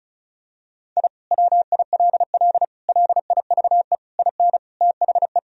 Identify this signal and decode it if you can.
Morse code, heard as a keyed tone, about 35 wpm, 700 Hz. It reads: I WILL LIVE IN THE
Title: Sentences from Top 1000 Words - Review Part 2 - 35wpm